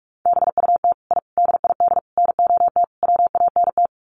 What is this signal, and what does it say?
Morse code, heard as a keyed tone, about 44 wpm, 700 Hz. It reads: BUT I DID NOT WANT
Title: Sets of 5 Words Encore 45wpm